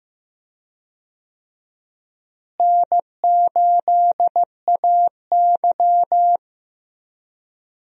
Morse code, heard as a keyed tone, about 15 wpm, 700 Hz. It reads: N8AY
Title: USA Call Signs - 4 Characters - Repeated 3x - 15wpm